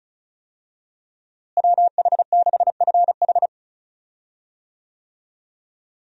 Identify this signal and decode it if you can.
Morse code, heard as a keyed tone, about 34 wpm, 700 Hz. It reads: WH6FH
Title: USA Call Signs - 5 Characters - Repeated 3x - 35wpm